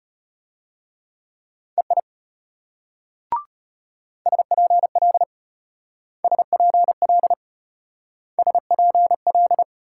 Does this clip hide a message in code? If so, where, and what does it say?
Yes: Morse code from 1.8 to 9.6 s, heard as a keyed tone, about 32 wpm, 700 Hz. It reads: EI SPL SPL SPL